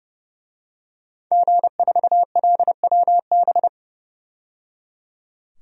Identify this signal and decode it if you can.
Morse code, heard as a keyed tone, about 30 wpm, 700 Hz. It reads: G4LWB
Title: DXCC-100 Call Signs - 5 Characters - 30wpm